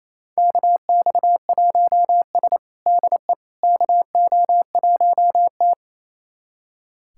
Morse code, heard as a keyed tone, about 27 wpm, 700 Hz. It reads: KX1S DE KO1T
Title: USA Callsign DE Callsign - 4 Characters - 28wpm